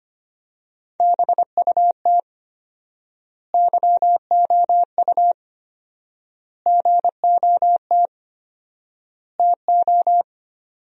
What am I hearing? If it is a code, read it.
Morse code, heard as a keyed tone, about 25 wpm, 700 Hz. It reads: BUT YOU GOT TO
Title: 4x Word-Spacing - Sentences from Top 300 Words - 25wpm